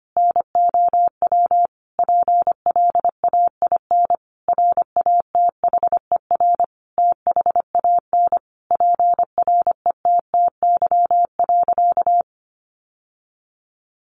Morse code, heard as a keyed tone, about 25 wpm, 700 Hz. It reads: NOW PLAIN RATHER THAN PRETTY.
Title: Sentences from Top 1000 Words 25wpm